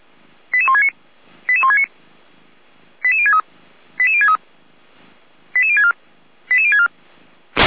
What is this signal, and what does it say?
Five-tone selective call (ZVEI), heard as quick runs of five tones, tones of about 80 ms. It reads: 88169 88169 88063 88063 88064 88064
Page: ZVEI